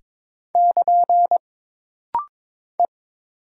Morse code, heard as a keyed tone, about 22 wpm, 700 Hz. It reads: <KN> E
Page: This rapid-fire timing is geared to push your ICR (Instant Character Recognition) skills.